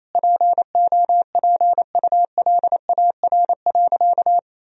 Morse code, heard as a keyed tone, about 28 wpm, 700 Hz. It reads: POPULAR.
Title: Sentences from Top 1000 Words - Review Part 3 - 28wpm